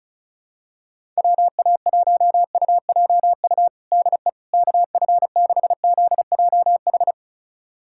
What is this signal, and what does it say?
Morse code, heard as a keyed tone, about 36 wpm, 700 Hz. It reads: WA1UJU DE KF6ZJH
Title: USA Callsign DE Callsign 35wpm